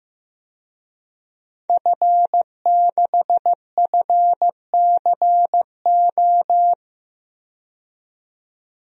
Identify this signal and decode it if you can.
Morse code, heard as a keyed tone, about 16 wpm, 700 Hz. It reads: F6FCO
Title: DXCC-100 Call Signs - 5 Characters - 15wpm